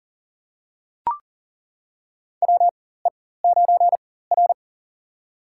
Morse code, heard as a keyed tone, about 40 wpm, spaced as for 19 wpm, 700 Hz. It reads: WE9R
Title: USA Call Signs - 4 Characters - 4x Character Spacing - 40wpm